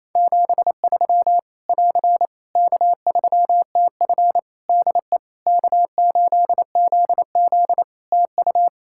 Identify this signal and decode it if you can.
Morse code, heard as a keyed tone, about 28 wpm, 700 Hz. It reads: 73 <AR> K3TF DE K8ZZ TU
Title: K1USN SST QSO Practice 28wpm